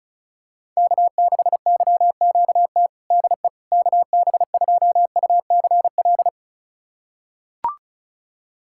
Morse code, heard as a keyed tone, about 34 wpm, 700 Hz. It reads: K6YQT DE KB2UCL
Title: USA Callsign DE Callsign 35wpm